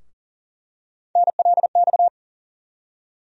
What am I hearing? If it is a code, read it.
Morse code, heard as a keyed tone, about 40 wpm, 700 Hz. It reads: NLX
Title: USA Call Sign Postfixes - 3 Letters 40wpm